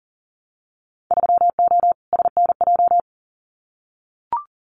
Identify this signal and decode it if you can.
Morse code, heard as a keyed tone, about 40 wpm, 700 Hz. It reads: 3O SNJ
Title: Winter Field Day Exchanges Encore 40wpm